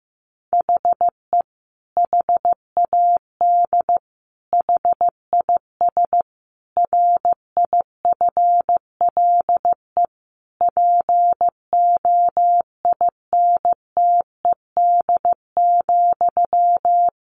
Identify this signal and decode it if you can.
Morse code, heard as a keyed tone, about 15 wpm, 700 Hz. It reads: HE HAD HIS RIFLE POINTED,
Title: A Sound Of Thunder 15wpm